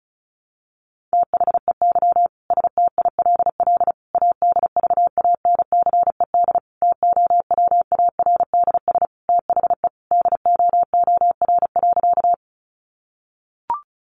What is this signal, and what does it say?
Morse code, heard as a keyed tone, about 34 wpm, 700 Hz. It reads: THEY STILL ADVANCED TOWARDS THE DOOR.
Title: Sentences from Top 1200 Words 35wpm